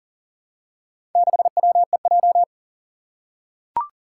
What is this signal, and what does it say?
Morse code, heard as a keyed tone, about 41 wpm, 700 Hz. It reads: 6WEJ
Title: Four Challenging Letter-Number Combinations 40wpm